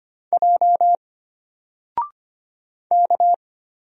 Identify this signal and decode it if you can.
Morse code, heard as a keyed tone, about 25 wpm, 700 Hz. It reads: J K
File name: Single-Letter-Number-25wpm